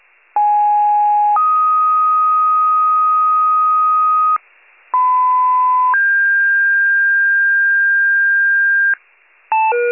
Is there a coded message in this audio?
Yes: two-tone page from 0.4 to 8.9 s, heard as a one-second tone, then a three-second tone, about 830 then 1250 Hz, about 1010 then 1600 Hz, with noise.